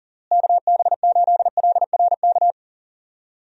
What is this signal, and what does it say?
Morse code, heard as a keyed tone, about 40 wpm, 700 Hz. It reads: KB8LRK
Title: USA Call Signs - 6 Characters - Repeated 3x - 40wpm